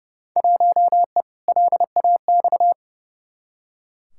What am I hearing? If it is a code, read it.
Morse code, heard as a keyed tone, about 31 wpm, 700 Hz. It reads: 1E LAX
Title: Field Day Exchanges - 30wpm